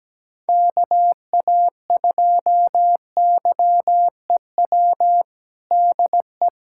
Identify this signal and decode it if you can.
Morse code, heard as a keyed tone, about 17 wpm, 700 Hz. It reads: KA2YEW DE
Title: USA Callsign DE Callsign - 6 Characters - 17wpm